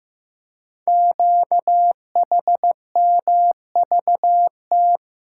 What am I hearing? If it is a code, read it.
Morse code, heard as a keyed tone, about 15 wpm, 700 Hz. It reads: QHMVT
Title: Code Groups - 5 Letters - 15wpm